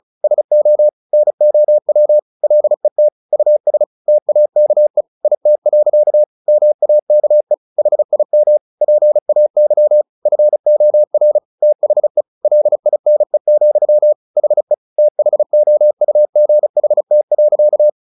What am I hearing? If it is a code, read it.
Morse code, heard as a keyed tone, about 37 wpm, 600 Hz. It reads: SO NOW LET US TAKE IT. MAKE HIM PAY FOR THE LINE, HE THOUGHT.
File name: the old man and the sea 2 35wpm